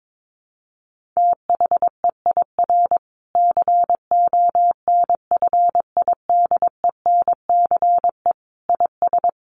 Morse code, heard as a keyed tone, about 22 wpm, 700 Hz. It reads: THEIR CONFIDENCE IS
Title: Sentences from Top 1500 Words 22wpm